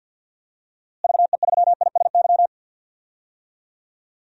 Morse code, heard as a keyed tone, about 53 wpm, 700 Hz. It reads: VE3ISY
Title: DXCC-100 Call Signs - 6 Characters - Repeated 2x - 50wpm